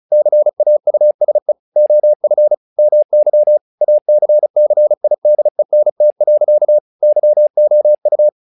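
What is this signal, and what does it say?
Morse code, heard as a keyed tone, about 35 wpm, 600 Hz. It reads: CAUSE OF MY ACCIDENT. YOU
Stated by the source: Morse practice text "Tower Trouble" sent at 35 WPM, from the SKCC - Straight Key Century Club learning center audio files.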